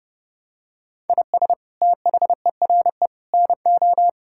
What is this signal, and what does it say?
Morse code, heard as a keyed tone, about 32 wpm, 700 Hz. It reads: IS THERE NO
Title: Sentences from Top 1000 Words - Review Part 5 - 30wpm